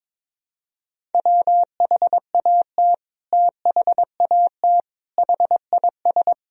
Morse code, heard as a keyed tone, about 22 wpm, 700 Hz. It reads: WHAT THAT HIS
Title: Sentences from Top 1000 Words - Review Part 2 - 22wpm